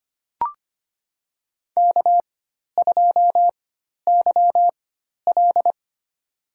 Morse code, heard as a keyed tone, about 25 wpm, spaced as for 12 wpm, 700 Hz. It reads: K2YL
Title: USA Call Signs - 4x Character Spacing - 25wpm